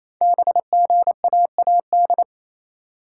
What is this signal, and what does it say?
Morse code, heard as a keyed tone, about 28 wpm, 700 Hz. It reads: BGAAD
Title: Code Groups - 5 Letters - 28wpm